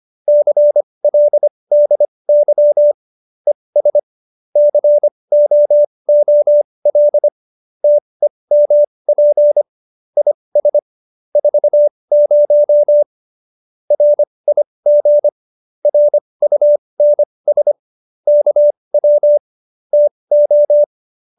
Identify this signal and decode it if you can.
Morse code, heard as a keyed tone, about 25 wpm, spaced as for 20 wpm, 590 Hz. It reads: CLDY ES COOL TEMP IS 40 RIG RUNS KW TO